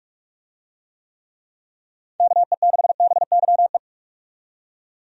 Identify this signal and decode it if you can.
Morse code, heard as a keyed tone, about 45 wpm, 700 Hz. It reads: KE6BYE
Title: USA Call Signs - 6 Characters - 45wpm